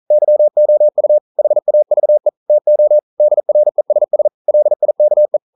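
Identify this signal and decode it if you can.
Morse code, heard as a keyed tone, about 39 wpm, 600 Hz. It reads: YOU HAVE TO DRESS LIKE
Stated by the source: Morse code practice at 41 wpm, from CW med Gnister.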